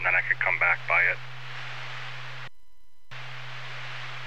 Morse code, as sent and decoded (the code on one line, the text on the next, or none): none